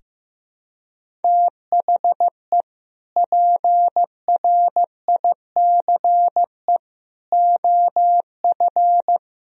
Morse code, heard as a keyed tone, about 15 wpm, 700 Hz. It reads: THE PRICE OF
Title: Sentences from Top 1000 Words - Review Part 4 - 15wpm